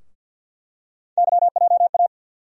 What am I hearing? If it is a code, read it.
Morse code, heard as a keyed tone, about 50 wpm, 700 Hz. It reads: YJA
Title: USA Call Sign Postfixes - 3 Letters 50wpm